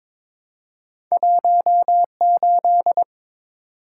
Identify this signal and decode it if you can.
Morse code, heard as a keyed tone, about 22 wpm, 700 Hz. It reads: 18
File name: practice / 22wpm / Two Letter-Number Combinations - 22wpm